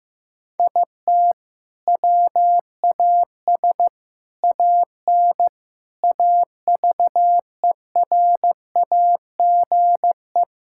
Morse code, heard as a keyed tone, about 15 wpm, 700 Hz. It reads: IT WAS AN AVERAGE
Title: Sentences from Top 1000 Words - Review Part 6 - 15wpm